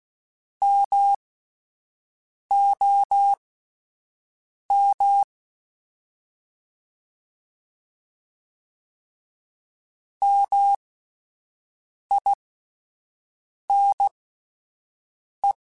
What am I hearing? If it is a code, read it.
Morse code, heard as a keyed tone, about 16 wpm, spaced as for 5 wpm, 800 Hz. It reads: MOM MINE